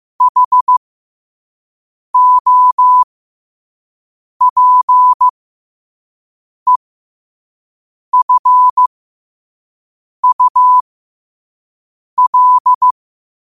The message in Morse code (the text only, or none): HOPEFUL